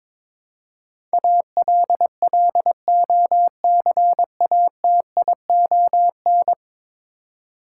Morse code, heard as a keyed tone, about 22 wpm, 700 Hz. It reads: ALLOCATION
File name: practice / 22wpm / 10 Letter Words - First 3 Letters Given - 22wpm